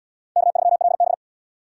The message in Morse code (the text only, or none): K3LB